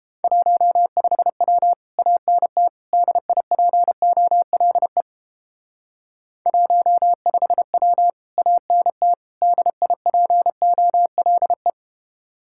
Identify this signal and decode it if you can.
Morse code, heard as a keyed tone, about 29 wpm, 700 Hz. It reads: 15W ANT DIPOLE 15W ANT DIPOLE